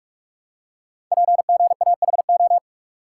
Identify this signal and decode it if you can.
Morse code, heard as a keyed tone, about 44 wpm, 690 Hz. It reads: PGAHO
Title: Code Groups - 5 Letters - 45wpm